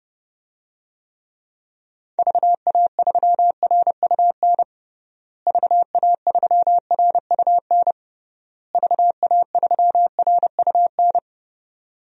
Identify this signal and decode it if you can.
Morse code, heard as a keyed tone, about 32 wpm, 700 Hz. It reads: VA3RUN VA3RUN VA3RUN
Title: DXCC-100 Call Signs - 6 Characters - Repeated 3x - 30wpm